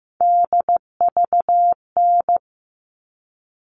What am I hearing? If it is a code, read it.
Morse code, heard as a keyed tone, about 15 wpm, 700 Hz. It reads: DVN
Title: All Cut Numbers - 3-Digit 15wpm